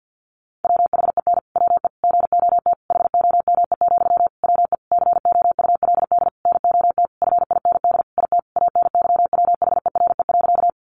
Morse code, heard as a keyed tone, about 50 wpm, 700 Hz. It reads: WHEN WE GOT HOME, WE COULD NOT FIND IT ANYWHERE.
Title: Sentences from Top 1400 Words 50wpm